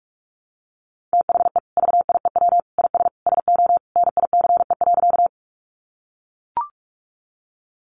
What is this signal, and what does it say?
Morse code, heard as a keyed tone, about 44 wpm, 700 Hz. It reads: THE VIEW IS SO NICE.
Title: Additional Easy Phrases 45wpm